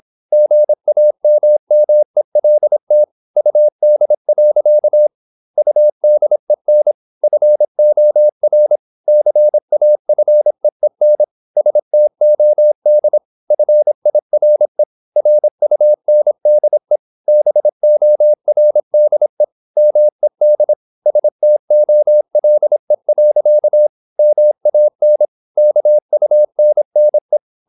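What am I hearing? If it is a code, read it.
Morse code, heard as a keyed tone, about 26 wpm, 600 Hz. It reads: GAMMELT UD. UDEN FOR CAFEEN STOD FIRE RUNDE BORDE MED STOLE. MAN KUNNE
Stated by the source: High-speed (QRQ) Morse at 26 wpm.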